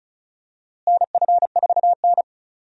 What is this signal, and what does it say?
Morse code, heard as a keyed tone, about 36 wpm, 700 Hz. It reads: NF4N
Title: USA Call Signs - 4 Characters - Repeated 2x - 35wpm